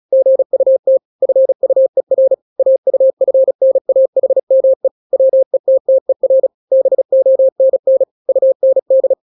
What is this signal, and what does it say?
Morse code, heard as a keyed tone, about 36 wpm, 530 Hz. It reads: GUT FUER AUFNAHME WETTER BONN UND
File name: daily / 35wpm